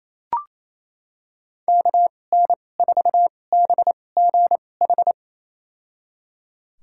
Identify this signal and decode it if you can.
Morse code, heard as a keyed tone, about 28 wpm, spaced as for 20 wpm, 700 Hz. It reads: KN4BGH
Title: USA Call Signs - 6 Characters - 2x Character Spacing - 28wpm